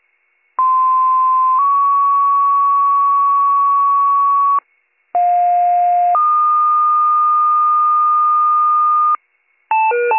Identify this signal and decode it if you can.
Two-tone page, heard as a one-second tone, then a three-second tone, about 1040 then 1120 Hz, about 710 then 1220 Hz. It is heard from 0.6 to 9.2 s.